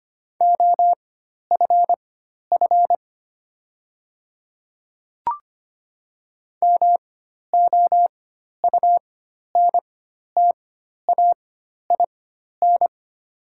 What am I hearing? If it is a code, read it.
Morse code, heard as a keyed tone, about 25 wpm, spaced as for 12 wpm, 700 Hz. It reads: OFF MOUNTAIN
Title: 4x Character Spacing - Top 300 Words - 25wpm